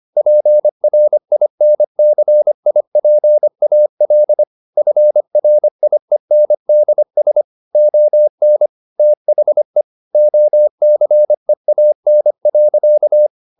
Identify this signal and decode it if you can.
Morse code, heard as a keyed tone, about 25 wpm, 600 Hz. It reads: PRINCIPAL FRIENDS ON THE OCEAN.
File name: the old man and the sea part 1 25wpm